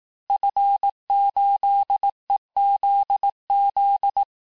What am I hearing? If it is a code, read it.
Morse code, heard as a keyed tone, about 18 wpm, 800 Hz. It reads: F8EZZ
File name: cours-du-lundi files / cours-1852-FIN-du-QSO-vitesse-17MPM